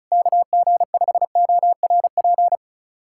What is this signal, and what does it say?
Morse code, heard as a keyed tone, about 36 wpm, 700 Hz. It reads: KG5ORP
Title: USA Callsign DE Callsign - 6 Characters - 35wpm